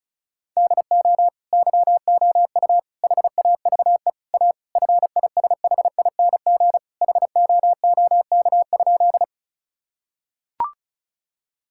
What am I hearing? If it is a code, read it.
Morse code, heard as a keyed tone, about 36 wpm, 700 Hz. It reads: DO YOU HAVE A FISHING HOOK?
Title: Sentences from Top 1600 Words 35wpm